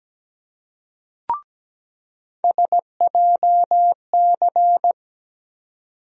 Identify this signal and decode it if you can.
Morse code, heard as a keyed tone, about 17 wpm, 700 Hz. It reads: SJC
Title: USA Call Sign Postfixes - 3 Letters 17wpm